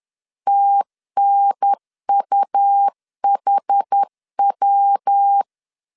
Morse code, heard as a keyed tone, about 10 wpm, 800 Hz. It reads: TNUHW
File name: CW / DET / 1 3 1500C S B 10wpm 3335